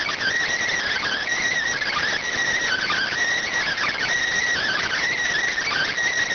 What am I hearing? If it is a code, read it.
Emergency Alert System (SAME) tones (header bursts), heard as harsh buzzing bursts. It is heard from 0.6 to 5.2 s.